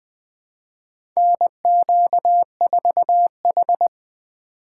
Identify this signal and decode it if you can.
Morse code, heard as a keyed tone, about 20 wpm, 700 Hz. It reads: NQ4H
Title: USA Call Signs - 4 Characters - 3x Character Spacing - 20wpm